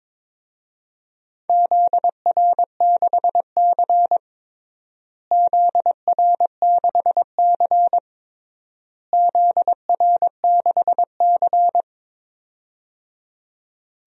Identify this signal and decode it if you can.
Morse code, heard as a keyed tone, about 22 wpm, 700 Hz. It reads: ZR6C ZR6C ZR6C
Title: DXCC-100 Call Signs - 4 Characters - Repeated 3x - 22wpm